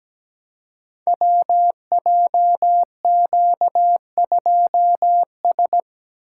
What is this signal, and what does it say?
Morse code, heard as a keyed tone, about 17 wpm, 700 Hz. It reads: WJQ2S
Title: Code Groups - 5 Letter:Numbers - 17wpm